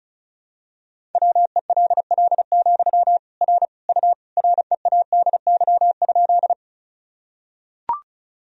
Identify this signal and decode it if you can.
Morse code, heard as a keyed tone, about 36 wpm, 700 Hz. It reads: WELL, R U READY?
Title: Sentences from Top 400 Words and Abbreviations 35wpm